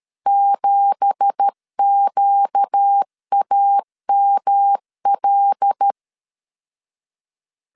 Morse code, heard as a keyed tone, about 13 wpm, 800 Hz. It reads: 7QAML
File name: CW / DET / 1 9 1500C S B 12wpm 3819